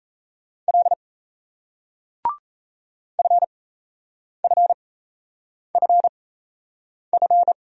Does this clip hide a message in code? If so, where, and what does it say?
Yes: Morse code from 0.7 to 7.5 s, heard as a keyed tone, about 38 wpm, spaced as for 8 wpm, 700 Hz. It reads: L FFFF